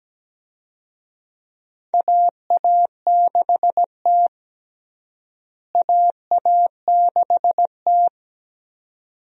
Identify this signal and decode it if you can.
Morse code, heard as a keyed tone, about 17 wpm, 700 Hz. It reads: AA6T AA6T
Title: USA Call Signs - 4 Characters - Repeated 2x - 17wpm